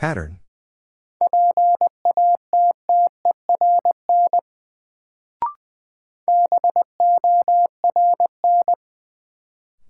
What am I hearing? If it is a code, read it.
Morse code, heard as a keyed tone, about 20 wpm, 700 Hz. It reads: PATTERN BORN
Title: Sentences from Top 1000 Words - New Word Drill - 20wpm